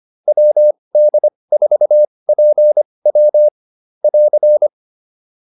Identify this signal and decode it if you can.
Morse code, heard as a keyed tone, about 25 wpm, spaced as for 20 wpm, 600 Hz. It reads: WD4PW <AR>
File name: QSO303a 20